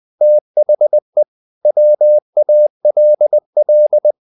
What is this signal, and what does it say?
Morse code, heard as a keyed tone, about 20 wpm, 600 Hz. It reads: THE WALL
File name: the old man and the sea / the old man and the sea part 1 20wpm